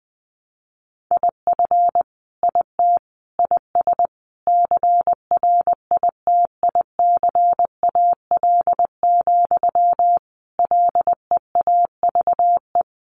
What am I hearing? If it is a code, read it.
Morse code, heard as a keyed tone, about 20 wpm, 700 Hz. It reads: IF IT IS CRITICAL, LEAVE
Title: Sentences from Top 1100 Words 20wpm